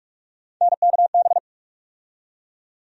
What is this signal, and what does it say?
Morse code, heard as a keyed tone, about 46 wpm, 700 Hz. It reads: NKB
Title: USA Call Sign Postfixes - 3 Letters 45wpm